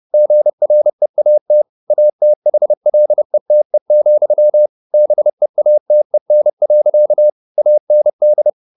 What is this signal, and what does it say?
Morse code, heard as a keyed tone, about 30 wpm, 600 Hz. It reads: GREAT ATHLETE, BEATEN. AND
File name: the old man and the sea 2 30wpm